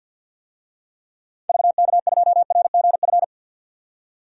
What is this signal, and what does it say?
Morse code, heard as a keyed tone, about 52 wpm, 700 Hz. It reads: VK2RGF
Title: DXCC-100 Call Signs - 6 Characters - 50wpm